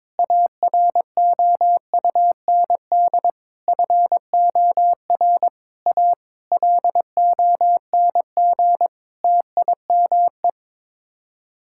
Morse code, heard as a keyed tone, about 22 wpm, 700 Hz. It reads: AROUND FOR A LONG TIME
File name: practice / 22wpm / Sets of 7 Words - 22wpm